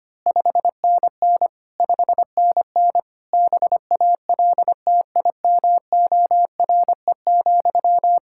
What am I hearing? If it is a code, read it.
Morse code, heard as a keyed tone, about 25 wpm, 700 Hz. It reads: 5NN 5NN BALTIMORE,
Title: Practice QSO - Exchange 1 - Beginning - 25wpm